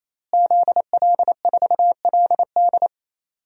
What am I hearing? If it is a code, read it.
Morse code, heard as a keyed tone, about 28 wpm, 690 Hz. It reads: ZL4LD
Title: DXCC-100 Call Signs - 5 Characters - Repeated 3x - 28wpm